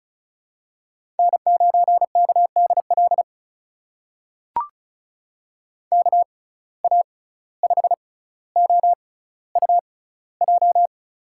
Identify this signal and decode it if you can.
Morse code, heard as a keyed tone, about 36 wpm, spaced as for 17 wpm, 700 Hz. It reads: N9KDL KA5OUJ
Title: USA Call Signs - 6x Character Spacing - 35wpm